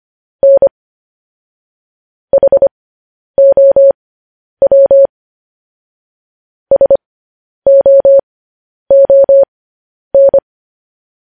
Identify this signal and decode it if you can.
Morse code, heard as a keyed tone, about 25 wpm, spaced as for 10 wpm, 550 Hz. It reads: N HOW SOON